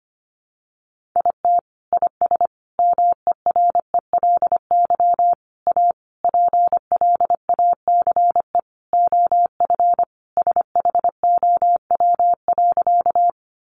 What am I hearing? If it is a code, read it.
Morse code, heard as a keyed tone, about 25 wpm, 700 Hz. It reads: IT IS MERELY A PLACE OF SHOW.
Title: Sentences from Top 1400 Words 25wpm